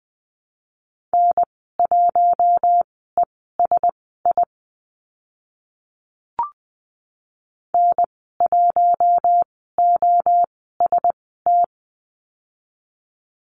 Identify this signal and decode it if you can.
Morse code, heard as a keyed tone, about 19 wpm, spaced as for 14 wpm, 700 Hz. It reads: N1ESI N1OST
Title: 042 Lesson 8g - Number 1 - Call Signs 20wpm